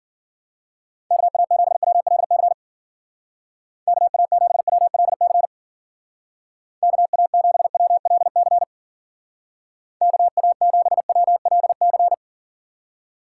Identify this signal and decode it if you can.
Morse code, heard as a keyed tone, about 49 wpm, 700 Hz. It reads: KA7WLC KA7WLC KA7WLC KA7WLC
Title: Speed Racing - USA 2x3 Call Signs 40wpm